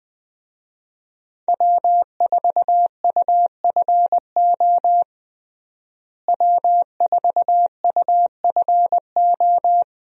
Morse code, heard as a keyed tone, about 21 wpm, 700 Hz. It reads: W4UFO W4UFO
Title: USA Call Signs - 5 Characters - Repeated 2x - 20wpm